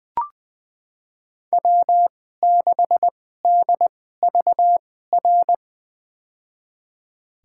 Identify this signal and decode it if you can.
Morse code, heard as a keyed tone, about 21 wpm, spaced as for 15 wpm, 700 Hz. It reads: W6DVR
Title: USA Call Signs - 5 Characters - 2x Character Spacing - 20wpm